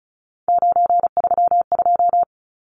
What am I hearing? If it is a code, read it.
Morse code, heard as a keyed tone, about 34 wpm, 700 Hz. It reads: 932
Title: Numbers - 3-Digit 35wpm